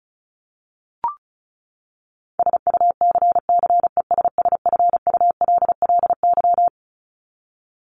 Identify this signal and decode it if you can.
Morse code, heard as a keyed tone, about 34 wpm, 700 Hz. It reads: SUCCESSFULLY
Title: LY Suffix - Top 100 Words with Base Words 35wpm